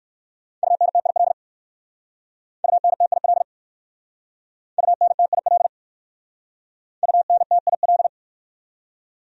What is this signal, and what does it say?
Morse code, heard as a keyed tone, about 59 wpm, 700 Hz. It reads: UNTIL UNTIL UNTIL UNTIL